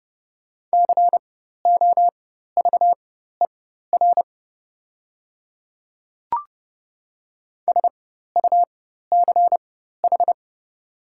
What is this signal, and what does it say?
Morse code, heard as a keyed tone, about 30 wpm, spaced as for 14 wpm, 700 Hz. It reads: COVER SUCH
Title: 4x Character Spacing - Top 300 Words - 30wpm